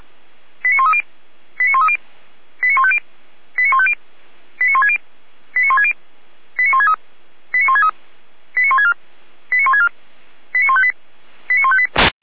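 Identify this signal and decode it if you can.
Five-tone selective call (ZVEI), heard as quick runs of five tones, tones of about 70 ms. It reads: 89130 89130 89150 89150 89160 89160 89163 89163 89164 89164 89158 89158
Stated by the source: Five-tone ZVEI selective call received by SDR.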